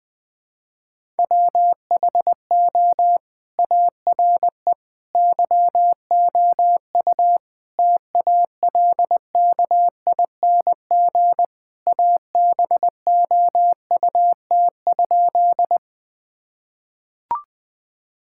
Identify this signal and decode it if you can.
Morse code, heard as a keyed tone, about 20 wpm, 700 Hz. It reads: WHO ARE YOU TALKING ABOUT?
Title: Sentences with 2-Word Phrases - 20wpm